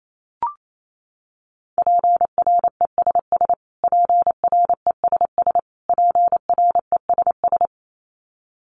Morse code, heard as a keyed tone, about 28 wpm, 700 Hz. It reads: PRESS PRESS PRESS
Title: Top 500 Words Repeated 28wpm